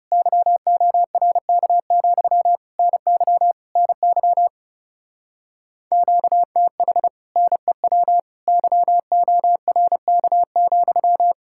Morse code, heard as a keyed tone, about 39 wpm, 700 Hz. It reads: YORK, NY NY QTH NEW YORK,